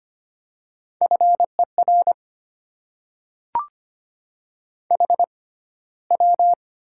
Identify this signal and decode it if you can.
Morse code, heard as a keyed tone, about 25 wpm, spaced as for 15 wpm, 700 Hz. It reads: FER HW
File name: Top 20 QSO Elements - 6x Character Spacing - 25wpm